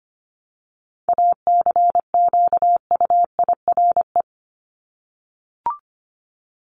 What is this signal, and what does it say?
Morse code, heard as a keyed tone, about 25 wpm, 700 Hz. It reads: ACQUIRE
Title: Top 1501-1750 Words 25wpm